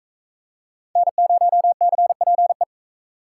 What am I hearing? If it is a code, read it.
Morse code, heard as a keyed tone, about 44 wpm, 700 Hz. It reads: N0CPE